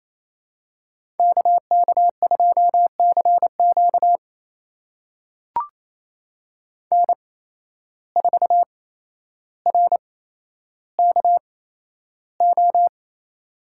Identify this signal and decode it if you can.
Morse code, heard as a keyed tone, about 28 wpm, spaced as for 12 wpm, 700 Hz. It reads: KK2CQ N4RKO
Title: USA Call Signs - 5 Characters - 8x Character Spacing - 28wpm